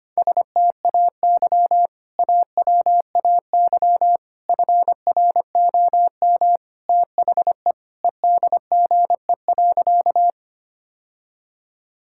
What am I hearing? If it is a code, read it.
Morse code, heard as a keyed tone, about 25 wpm, 700 Hz. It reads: STAY AWAY FROM THE EDGE.
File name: practice / 25wpm / Sentences with 3-Word Phrases - 25wpm